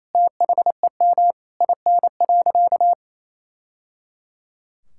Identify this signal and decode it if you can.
Morse code, heard as a keyed tone, about 28 wpm, 700 Hz. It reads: THEM IN.
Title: Sentences from Top 1000 Words - Review Part 6 - 28wpm